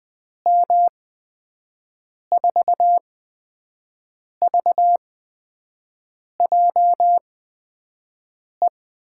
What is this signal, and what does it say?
Morse code, heard as a keyed tone, about 20 wpm, spaced as for 5 wpm, 700 Hz. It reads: M4VJE
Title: USA Call Signs - 6 Characters - 8x Character Spacing - 20wpm